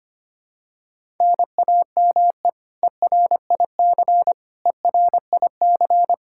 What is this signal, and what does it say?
Morse code, heard as a keyed tone, about 25 wpm, 700 Hz. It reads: NAME ERIC ERIC
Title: Practice QSO - Exchange 1 - Beginning - 25wpm